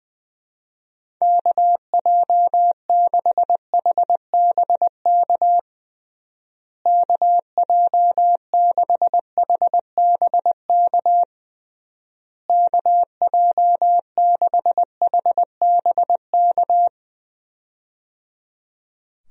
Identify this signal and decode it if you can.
Morse code, heard as a keyed tone, about 21 wpm, 700 Hz. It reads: KJ6HBK KJ6HBK KJ6HBK
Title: USA Call Signs - 6 Characters - Repeated 3x - 20wpm